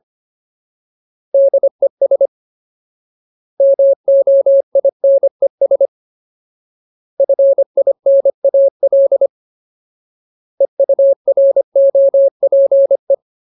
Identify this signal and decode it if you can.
Morse code, heard as a keyed tone, about 25 wpm, 550 Hz. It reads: DES MOINES FINAL EUROPE
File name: Ultimate Word List / 25WPM Ultimate